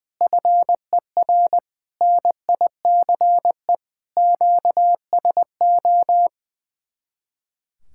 Morse code, heard as a keyed tone, about 21 wpm, 700 Hz. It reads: FER NICE QSO
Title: Speed-Racing Common QSO Phrases - 20wpm